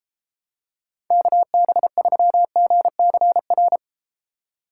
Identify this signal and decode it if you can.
Morse code, heard as a keyed tone, about 33 wpm, 700 Hz. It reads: KB3GCR